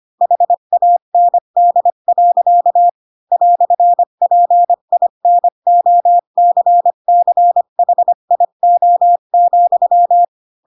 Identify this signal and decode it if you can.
Morse code, heard as a keyed tone, about 25 wpm, 700 Hz. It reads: HAND. "PINOCCHIO,
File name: Pinocchio / Pinocchio ch3 20 25 ik1hge